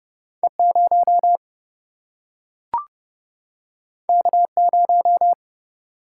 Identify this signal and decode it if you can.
Morse code, heard as a keyed tone, about 30 wpm, 700 Hz. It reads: E0 K0
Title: Single-Letter Single-Number 30wpm